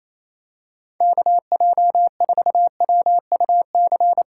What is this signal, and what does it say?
Morse code, heard as a keyed tone, about 28 wpm, 700 Hz. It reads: KJ4WUC
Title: USA Call Signs - 6 Characters - 4x Character Spacing - 28wpm